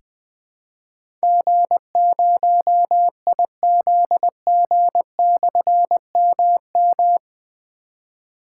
Morse code, heard as a keyed tone, about 20 wpm, 700 Hz. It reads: G0IZG/MM
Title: Difficult Call Signs 20wpm